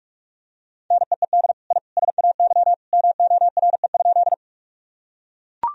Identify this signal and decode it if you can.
Morse code, heard as a keyed tone, about 41 wpm, 700 Hz. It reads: NEED I SAY MORE?
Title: Sentences from Top 1000 Words - Review Part 1 - 45wpm